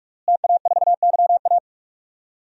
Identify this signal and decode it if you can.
Morse code, heard as a keyed tone, about 46 wpm, 700 Hz. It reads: TA4YA
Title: DXCC-100 Call Signs - 5 Characters - Repeated 3x - 45wpm